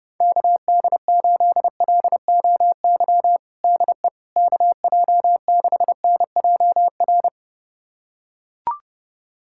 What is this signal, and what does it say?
Morse code, heard as a keyed tone, about 30 wpm, 700 Hz. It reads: KD8LOY DE KJ6NJR
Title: USA Callsign DE Callsign - 6 Characters - 30wpm